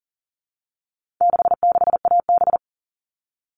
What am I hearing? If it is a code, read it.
Morse code, heard as a keyed tone, about 40 wpm, 700 Hz. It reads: 66AB
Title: All Cut Numbers - 4-Digit 40wpm